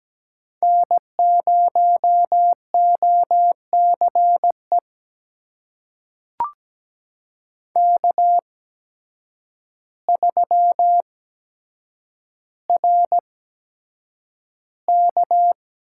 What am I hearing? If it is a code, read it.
Morse code, heard as a keyed tone, about 17 wpm, spaced as for 8 wpm, 700 Hz. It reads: N0OCE K3RK
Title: USA Call Signs - 5 Characters - 8x Character Spacing - 17wpm